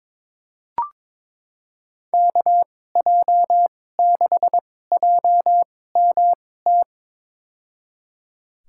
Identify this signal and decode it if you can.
Morse code, heard as a keyed tone, about 22 wpm, spaced as for 16 wpm, 700 Hz. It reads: KJ6JMT
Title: USA Call Signs - 6 Characters - 2x Character Spacing - 22wpm